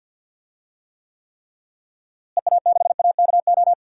Morse code, heard as a keyed tone, about 45 wpm, 700 Hz. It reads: EA6AKO
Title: DXCC-100 Call Signs - 6 Characters - Repeated 3x - 50wpm